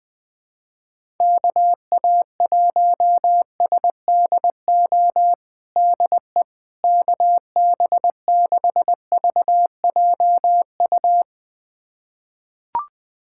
Morse code, heard as a keyed tone, about 21 wpm, 700 Hz. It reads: KA1SDO DE KB6VJU
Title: USA Callsign DE Callsign - 6 Characters - 20wpm